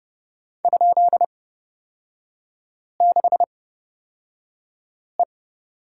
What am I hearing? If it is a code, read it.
Morse code, heard as a keyed tone, about 30 wpm, spaced as for 5 wpm, 700 Hz. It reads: ?6E
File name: Lesson 38e - Letter q - Three Characters 30wpm